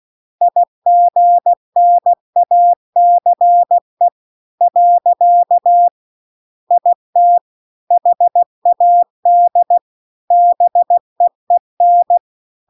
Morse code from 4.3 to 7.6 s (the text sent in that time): . IT